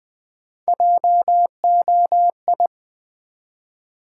Morse code, heard as a keyed tone, about 20 wpm, 700 Hz. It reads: JOI
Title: Build A Word - 1 Letter At A Time - 4-Letter Words - 20wpm